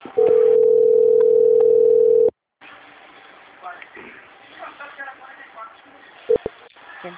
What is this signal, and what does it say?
Ringback tone (North America), 440 + 480 Hz, with speech.
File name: out- - - - - .91724